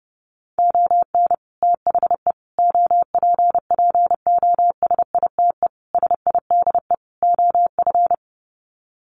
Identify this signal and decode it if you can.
Morse code, heard as a keyed tone, about 30 wpm, 700 Hz. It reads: ON THE OPPOSITE SIDE OF
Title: Sets of 5 Words Encore2 30wpm